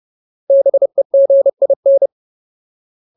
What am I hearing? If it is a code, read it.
Morse code, heard as a keyed tone, about 30 wpm, 550 Hz. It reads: BEGIN
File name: Ultimate Word List / 30WPM Ultimate